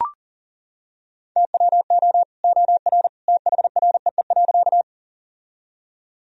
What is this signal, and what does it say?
Morse code, heard as a keyed tone, about 40 wpm, 700 Hz. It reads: TWO OR THREE.
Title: Sentences from Top 1000 Words - Review Part 1 - 40wpm